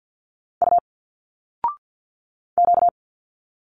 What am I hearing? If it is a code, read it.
Morse code, heard as a keyed tone, about 51 wpm, 700 Hz. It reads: U Q
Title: Single Letters 50wpm